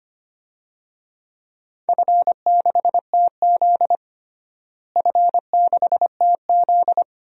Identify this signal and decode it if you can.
Morse code, heard as a keyed tone, about 25 wpm, 700 Hz. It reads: F6TZ F6TZ
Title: DXCC-100 Call Signs - 4 Characters - Repeated 2x - 25wpm